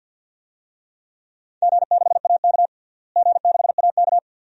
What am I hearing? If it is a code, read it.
Morse code, heard as a keyed tone, about 51 wpm, 700 Hz. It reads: G6AK G6AK
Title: DXCC-100 Call Signs - 4 Characters - Repeated 2x - 50wpm